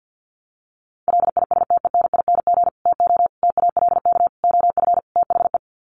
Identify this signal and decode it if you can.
Morse code, heard as a keyed tone, about 50 wpm, 700 Hz. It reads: LISTENING TO TALK OF THE
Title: Sets of 5 Words 50wpm